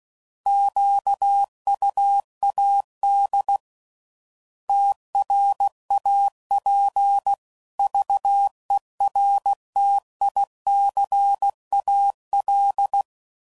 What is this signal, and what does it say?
Morse code, heard as a keyed tone, about 16 wpm, 800 Hz. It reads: QUAD TRAPVERTICAL
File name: CW Code Groups @15WPM / fromSM's / Antennas